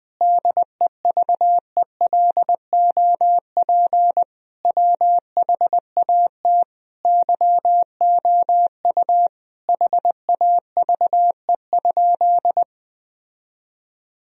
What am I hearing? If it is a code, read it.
Morse code, heard as a keyed tone, about 21 wpm, 700 Hz. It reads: DEVELOP WHAT YOU HAVE?
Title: Sentences from Top 1000 Words - Review Part 2 - 20wpm